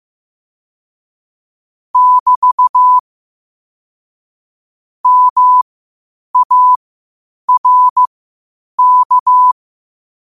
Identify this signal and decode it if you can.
Morse code, heard as a keyed tone, about 15 wpm, spaced as for 8 wpm, 1000 Hz. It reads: <BT> MARK
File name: qotdfiles / 8wpm